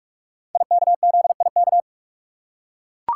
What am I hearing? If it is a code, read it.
Morse code, heard as a keyed tone, about 44 wpm, 700 Hz. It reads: IKZIK
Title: Code Groups - 5 Letters - 45wpm